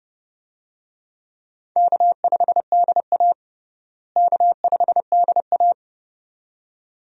Morse code, heard as a keyed tone, about 30 wpm, 700 Hz. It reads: K5DA K5DA
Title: USA Call Signs - 4 Characters - Repeated 2x - 30wpm